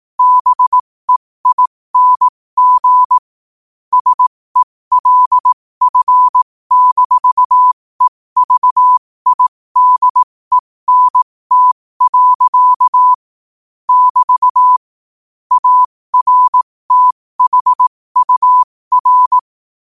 Morse code, heard as a keyed tone, about 18 wpm, spaced as for 15 wpm, 1000 Hz. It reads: BEING SELF-EVIDENT. <BT> ARTHUR
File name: qotdfiles / 15wpm